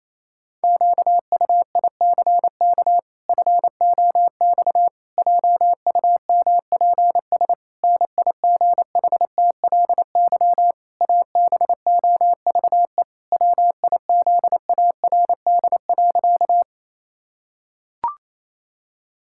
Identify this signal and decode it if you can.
Morse code, heard as a keyed tone, about 28 wpm, 700 Hz. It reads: QUICK FOX JUMPS NIGHTLY ABOVE WIZARD.
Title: Pangram Challenge v2 - 28wpm